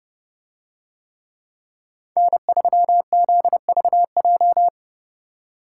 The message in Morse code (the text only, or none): N3ZVJ